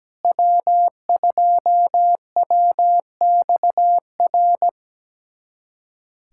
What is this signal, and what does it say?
Morse code, heard as a keyed tone, about 17 wpm, 700 Hz. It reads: W2WXR
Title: USA Callsign DE Callsign - 5 Characters - 17wpm